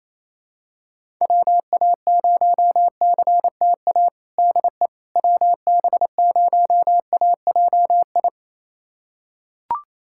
Morse code, heard as a keyed tone, about 28 wpm, 700 Hz. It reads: WA0CTA DE WB0AJI
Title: USA Callsign DE Callsign - 6 Characters - 28wpm